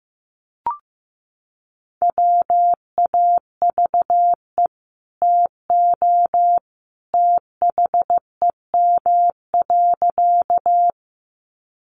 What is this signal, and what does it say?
Morse code, heard as a keyed tone, about 15 wpm, 700 Hz. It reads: WAVE TO THEM.
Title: Sentences from Top 1400 Words 15wpm